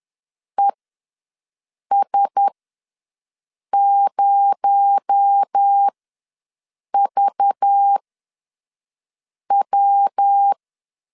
Morse code, heard as a keyed tone, about 10 wpm, spaced as for 5 wpm, 800 Hz. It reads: ES0VW
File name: LEZIONI / 7 GF9Z / 7 GF9 ws 10wpm 4034